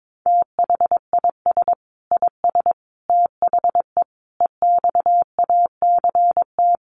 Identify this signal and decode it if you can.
Morse code, heard as a keyed tone, about 22 wpm, 700 Hz. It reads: THIS IS THE EXACT
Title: Sentences from Top 1300 Words 22wpm